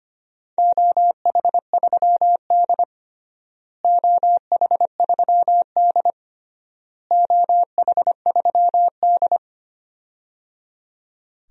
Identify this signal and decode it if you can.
Morse code, heard as a keyed tone, about 25 wpm, 700 Hz. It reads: OH3D OH3D OH3D
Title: DXCC-100 Call Signs - 4 Characters - Repeated 3x - 25wpm